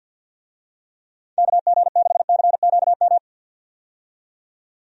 Morse code, heard as a keyed tone, about 49 wpm, 700 Hz. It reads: KG6CQM
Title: USA Call Signs - 6 Characters - 50wpm